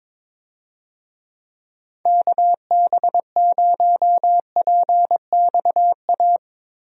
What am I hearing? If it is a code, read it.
Morse code, heard as a keyed tone, about 22 wpm, 700 Hz. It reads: KB0PXA
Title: USA Call Signs - 6 Characters - Repeated 3x - 22wpm